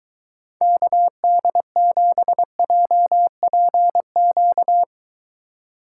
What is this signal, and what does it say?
Morse code, heard as a keyed tone, about 23 wpm, 700 Hz. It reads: KD7JPQ